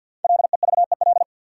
Morse code, heard as a keyed tone, about 54 wpm, 700 Hz. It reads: LEVEL
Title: Sentences from Top 700 Words - New Word Drill - 50wpm